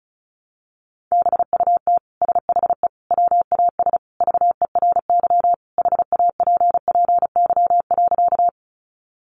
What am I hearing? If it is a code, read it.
Morse code, heard as a keyed tone, about 34 wpm, 700 Hz. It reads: BUT SHE WAS VERY HAPPY.
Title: Sentences from Top 1000 Words 35wpm